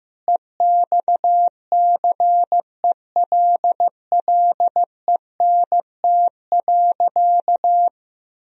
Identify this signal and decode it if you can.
Morse code, heard as a keyed tone, about 15 wpm, 700 Hz. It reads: EXCELLENT.
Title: Sentences from Top 1000 Words - Review Part 4 - 15wpm